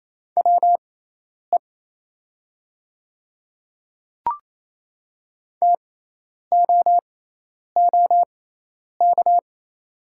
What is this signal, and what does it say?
Morse code, heard as a keyed tone, about 28 wpm, spaced as for 10 wpm, 700 Hz. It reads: WE TOOK
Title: 6x Character Spacing - Top 300 Words - 28wpm